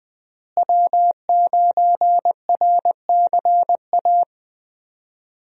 Morse code, heard as a keyed tone, about 20 wpm, 700 Hz. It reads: W9RCA
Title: USA Call Signs - 5 Characters - Repeated 2x - 20wpm